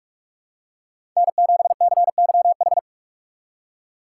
Morse code, heard as a keyed tone, about 46 wpm, 700 Hz. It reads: N7CYH
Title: USA Call Signs - 5 Characters - 3x Character Spacing - 45wpm